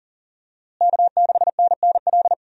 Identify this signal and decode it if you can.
Morse code, heard as a keyed tone, about 40 wpm, 700 Hz. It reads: K6NNL
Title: USA Call Signs - 5 Characters - Repeated 3x - 40wpm